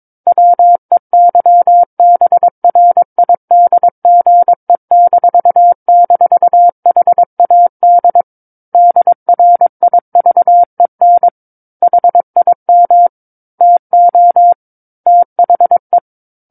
Morse code, heard as a keyed tone, about 22 wpm, 700 Hz. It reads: WEYBRIDGE--HAD DRIVEN HIM TO THE
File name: Book One / 13-Chapter 13 - 22 WPM